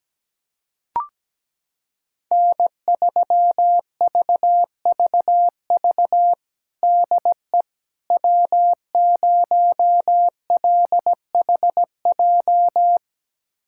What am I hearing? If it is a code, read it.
Morse code, heard as a keyed tone, about 17 wpm, 700 Hz. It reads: N3VVV DE W0LHJ
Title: USA Callsign DE Callsign - 5 Characters - 17wpm